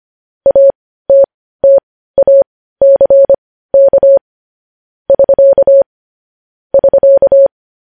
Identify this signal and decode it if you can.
Morse code, heard as a keyed tone, about 25 wpm, spaced as for 15 wpm, 550 Hz. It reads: ATTACK <SK> <SK>